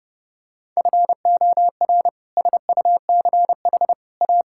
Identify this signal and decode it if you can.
Morse code, heard as a keyed tone, about 30 wpm, 700 Hz. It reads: FOR SUCH A
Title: Sentences from Top 1000 Words - Review Part 6 - 30wpm